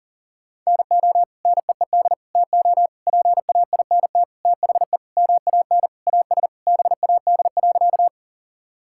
Morse code, heard as a keyed tone, about 37 wpm, 700 Hz. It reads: NO NEED TO PAINT THE MAN AS BAD.
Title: Sentences from Top 1000 Words - Review Part 6 - 40wpm